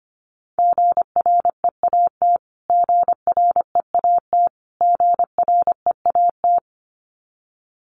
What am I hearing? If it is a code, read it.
Morse code, heard as a keyed tone, about 25 wpm, 700 Hz. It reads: GREAT GREAT GREAT
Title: Top 500 Words Repeated 25wpm